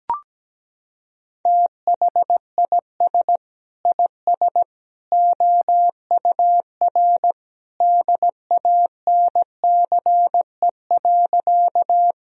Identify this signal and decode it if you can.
Morse code, heard as a keyed tone, about 17 wpm, 700 Hz. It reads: THIS IS OUR DANCE.
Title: Sentences from Top 1000 Words - Review Part 4 - 17wpm